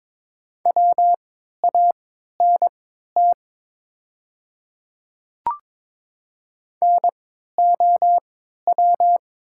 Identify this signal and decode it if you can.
Morse code, heard as a keyed tone, about 22 wpm, spaced as for 12 wpm, 700 Hz. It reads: WANT NOW
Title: 3x Character Spacing - Top 300 Words - 22wpm